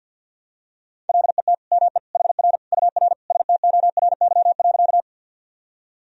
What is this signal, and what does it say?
Morse code, heard as a keyed tone, about 49 wpm, 700 Hz. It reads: LET ME HR UR STORY.
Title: Sentences from Top 300 Words and Abbreviations 50wpm